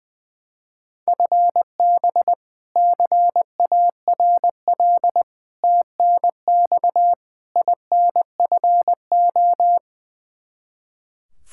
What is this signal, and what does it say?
Morse code, heard as a keyed tone, about 21 wpm, 700 Hz. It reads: FB CARL TNX INFO
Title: Practice QSO - Exchange 3 - More Info - 20wpm